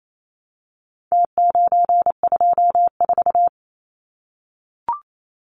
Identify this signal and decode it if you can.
Morse code, heard as a keyed tone, about 28 wpm, 700 Hz. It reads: T924
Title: Numbers - Contest Style - 4-Digit 28wpm